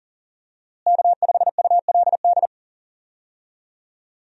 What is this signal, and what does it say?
Morse code, heard as a keyed tone, about 40 wpm, 700 Hz. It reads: K5ULD
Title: USA Call Signs - 5 Characters - Repeated 2x - 40wpm